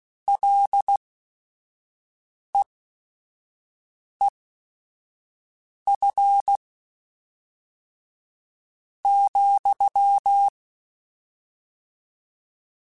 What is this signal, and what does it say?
Morse code, heard as a keyed tone, about 16 wpm, spaced as for 5 wpm, 800 Hz. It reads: LEEF ,
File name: Short Runs / Short Runs-mp3s / SinglePrefixes @5